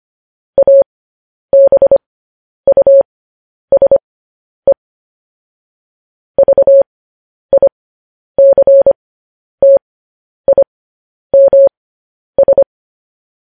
Morse code, heard as a keyed tone, about 25 wpm, spaced as for 10 wpm, 550 Hz. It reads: ABUSE VICTIMS
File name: podcast / News headlines-1x-25x10